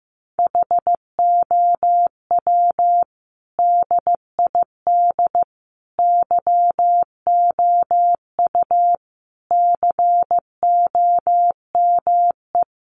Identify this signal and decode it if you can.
Morse code, heard as a keyed tone, about 15 wpm, 700 Hz. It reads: HOW DID YOU COME
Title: Sentences from Top 100 Words 15wpm